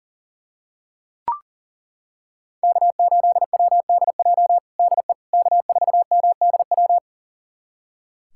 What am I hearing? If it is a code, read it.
Morse code, heard as a keyed tone, about 37 wpm, 700 Hz. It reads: K8WDJ DE K4MDW
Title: USA Callsign DE Callsign - 5 Characters - 40wpm